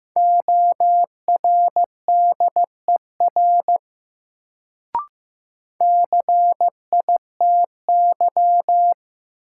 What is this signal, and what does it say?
Morse code, heard as a keyed tone, about 15 wpm, 700 Hz. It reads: ORDER CITY
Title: Sets of 2 Words - New Word Drill - 15wpm